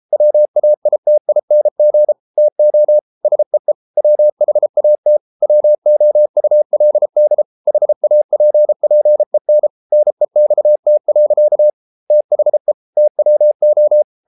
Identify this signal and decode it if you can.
Morse code, heard as a keyed tone, about 33 wpm, 600 Hz. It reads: WAITING TO SEE WHAT WOULD HAPPEN NEXT. THE TWO